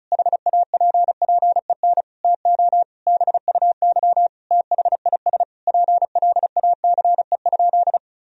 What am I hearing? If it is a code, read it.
Morse code, heard as a keyed tone, about 36 wpm, 700 Hz. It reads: HAPPEN TO BUY THIS PLACE?
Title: Sentences from Top 1000 Words - Review Part 4 - 35wpm